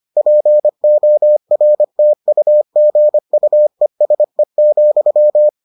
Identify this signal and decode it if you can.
Morse code, heard as a keyed tone, about 25 wpm, 600 Hz. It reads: PORTUGUESE,